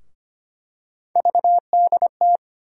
Morse code, heard as a keyed tone, about 25 wpm, 700 Hz. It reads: VDT
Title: USA Call Sign Postfixes - 3 Letters 25wpm